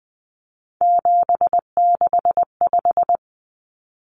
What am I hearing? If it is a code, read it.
Morse code, heard as a keyed tone, about 20 wpm, 700 Hz. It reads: 765
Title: Numbers - 3-Digit 20wpm